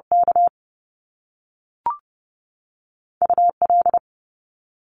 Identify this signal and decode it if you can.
Morse code, heard as a keyed tone, about 30 wpm, 700 Hz. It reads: K UL
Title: Top Two Letter Combinations in Words 30wpm